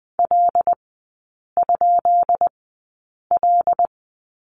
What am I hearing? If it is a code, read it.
Morse code, heard as a keyed tone, about 20 wpm, spaced as for 8 wpm, 700 Hz. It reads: L?L
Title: 184 Lesson 30e - Letter b - Three Characters 20wpm